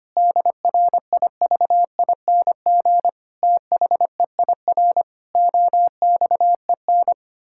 Morse code, heard as a keyed tone, about 25 wpm, 700 Hz. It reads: DRIVING THEIR OXEN
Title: Pangram Challenge v2 - 25wpm